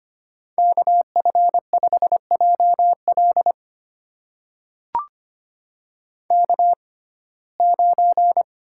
Morse code, heard as a keyed tone, about 25 wpm, spaced as for 18 wpm, 700 Hz. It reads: KF5JL K9
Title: USA Call Signs - 5 Characters - 6x Character Spacing - 25wpm